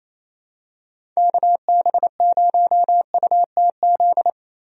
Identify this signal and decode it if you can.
Morse code, heard as a keyed tone, about 28 wpm, 700 Hz. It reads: KB0UTZ
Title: USA Call Signs - 6 Characters - Repeated 3x - 28wpm